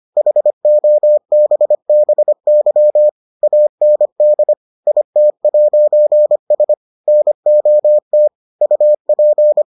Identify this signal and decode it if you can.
Morse code, heard as a keyed tone, about 25 wpm, 600 Hz. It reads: HOBBY AND IT'S NOT UP